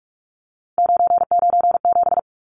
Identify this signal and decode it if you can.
Morse code, heard as a keyed tone, about 44 wpm, 700 Hz. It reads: 997
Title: Numbers - 3-Digit 45wpm